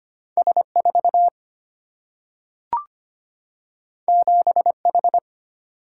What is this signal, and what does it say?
Morse code, heard as a keyed tone, about 25 wpm, 700 Hz. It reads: S 4 7 H
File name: Two Letter-Number Combinations - 25wpm